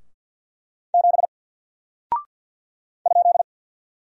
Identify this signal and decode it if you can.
Morse code, heard as a keyed tone, about 48 wpm, 700 Hz. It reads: 7 ?
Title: Single Character (Letter:Number:Punctuation:Prosign) Alt 50wpm